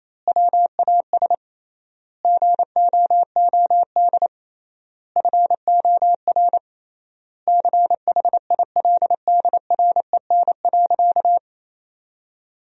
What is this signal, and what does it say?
Morse code, heard as a keyed tone, about 28 wpm, 700 Hz. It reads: WAS GOOD FOR CHILDREN.
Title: 3x Word-Spacing - Sentences from Top 400 Words - 28wpm